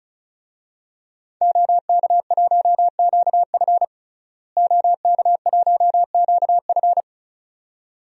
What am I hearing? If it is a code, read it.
Morse code, heard as a keyed tone, about 36 wpm, 700 Hz. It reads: OK1QF OK1QF
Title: DXCC-100 Call Signs - 5 Characters - Repeated 2x - 35wpm